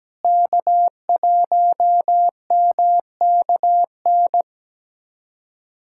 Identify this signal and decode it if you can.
Morse code, heard as a keyed tone, about 17 wpm, 700 Hz. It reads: K1MKN
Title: USA Callsign DE Callsign 17wpm